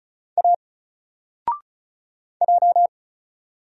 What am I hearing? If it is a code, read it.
Morse code, heard as a keyed tone, about 34 wpm, 700 Hz. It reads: A J
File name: Single-Letter-Number-35wpm